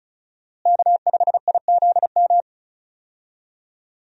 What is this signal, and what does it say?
Morse code, heard as a keyed tone, about 36 wpm, 700 Hz. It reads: K5IZM
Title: USA Call Signs - 5 Characters - Repeated 3x - 35wpm